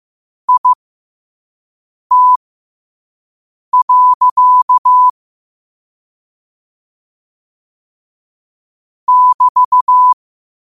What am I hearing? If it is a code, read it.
Morse code, heard as a keyed tone, about 15 wpm, spaced as for 5 wpm, 1000 Hz. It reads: IT. <BT>
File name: qotdfiles / 5wpm